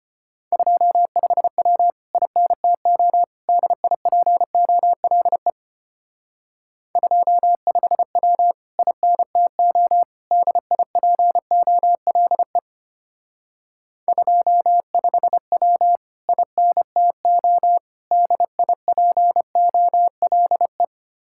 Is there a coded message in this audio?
Yes: Morse code from 0.5 to 20.8 s, heard as a keyed tone, about 32 wpm, 700 Hz. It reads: 25W INTO DIPOLE 25W INTO DIPOLE 25W INTO DIPOLE